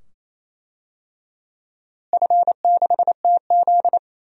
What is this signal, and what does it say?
Morse code, heard as a keyed tone, about 28 wpm, 700 Hz. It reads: F6TZ
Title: DXCC-100 Call Signs - 4 Characters - Repeated 2x - 28wpm